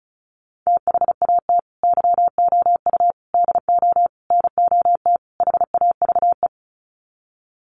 Morse code, heard as a keyed tone, about 34 wpm, 700 Hz. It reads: THAT YOU DO NOT HAVE
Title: Sets of 5 Words Encore 35wpm